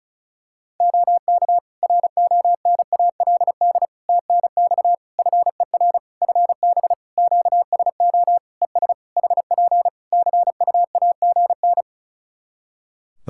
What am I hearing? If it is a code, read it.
Morse code, heard as a keyed tone, about 36 wpm, 700 Hz. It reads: OK RONALD TNX FER FB QSO ES HP CUAGN
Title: Practice QSO - Exchange 4 - Ending - 35wpm